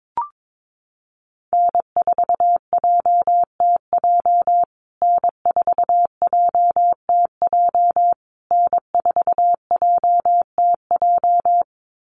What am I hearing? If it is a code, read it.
Morse code, heard as a keyed tone, about 22 wpm, 700 Hz. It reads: N4JTJ N4JTJ N4JTJ
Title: USA Call Signs Repeated 3x 22wpm